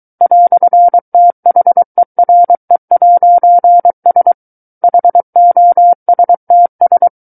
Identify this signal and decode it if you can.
Morse code, heard as a keyed tone, about 23 wpm, 700 Hz. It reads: "THERE'S HOSTS
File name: Book One / 14-Chapter 14 - 23 WPM